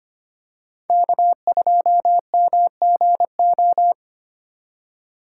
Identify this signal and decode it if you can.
Morse code, heard as a keyed tone, about 25 wpm, 700 Hz. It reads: K2MGO
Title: USA Call Signs - 5 Characters - 4x Character Spacing - 25wpm